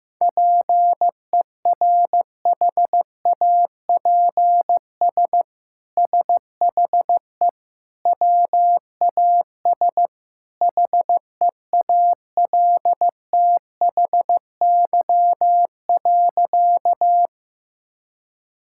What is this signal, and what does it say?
Morse code, heard as a keyed tone, about 15 wpm, 700 Hz. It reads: PERHAPS SHE WAS HEALTHY.
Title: Sentences from Top 1000 Words - Review Part 5 - 15wpm